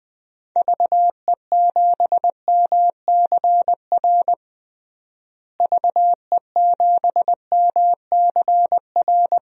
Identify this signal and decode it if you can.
Morse code, heard as a keyed tone, about 21 wpm, 700 Hz. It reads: VE7MCR VE7MCR
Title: DXCC-100 Call Signs - 6 Characters - Repeated 3x - 20wpm